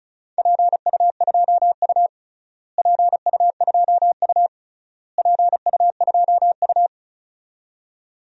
Morse code, heard as a keyed tone, about 36 wpm, 700 Hz. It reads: PU2U PU2U PU2U
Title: DXCC-100 Call Signs - 4 Characters - Repeated 3x - 35wpm